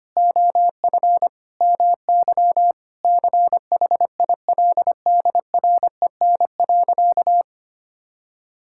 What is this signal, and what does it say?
Morse code, heard as a keyed tone, about 25 wpm, 700 Hz. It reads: OF MY CHILDREN.
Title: Sentences from Top 1000 Words - Review Part 5 - 25wpm